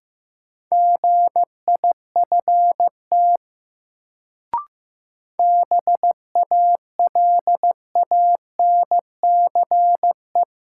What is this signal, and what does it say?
Morse code, heard as a keyed tone, about 15 wpm, 700 Hz. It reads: GIFT BALANCE
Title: Sentences from Top 900 Words - New Word Drill - 15wpm